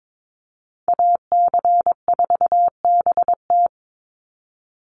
Morse code, heard as a keyed tone, about 22 wpm, 700 Hz. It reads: AC4BT
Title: Call Signs - 22wpm